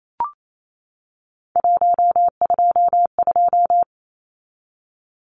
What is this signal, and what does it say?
Morse code, heard as a keyed tone, about 28 wpm, 700 Hz. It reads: 122
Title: Numbers - 3-Digit 28wpm